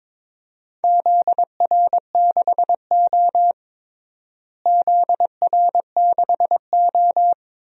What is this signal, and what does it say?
Morse code, heard as a keyed tone, about 22 wpm, 700 Hz. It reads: ZR6O ZR6O
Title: DXCC-100 Call Signs - 4 Characters - Repeated 2x - 22wpm